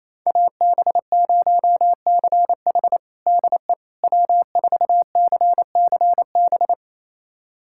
Morse code, heard as a keyed tone, about 28 wpm, 700 Hz. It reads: AB0CH DE W4CCB
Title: USA Callsign DE Callsign - 5 Characters - 28wpm